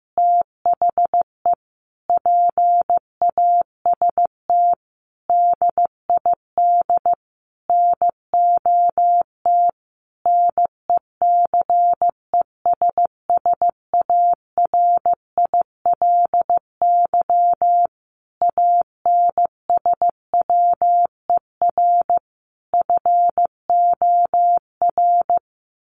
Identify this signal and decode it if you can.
Morse code, heard as a keyed tone, about 15 wpm, 700 Hz. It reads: THE PAST DID NOT NECESSARILY ANSWER FOR
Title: Sentences from Top 1400 Words 15wpm